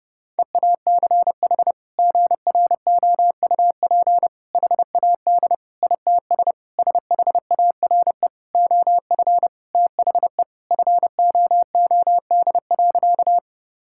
Morse code, heard as a keyed tone, about 32 wpm, 700 Hz. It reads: EACH GROUP HAD ITS SHARE OF THE FOOD.
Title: Sentences from Top 1000 Words - Review Part 6 - 30wpm